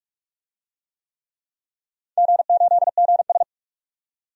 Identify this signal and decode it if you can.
Morse code, heard as a keyed tone, about 46 wpm, 700 Hz. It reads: G8GS
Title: DXCC-100 Call Signs - 4 Characters - 45wpm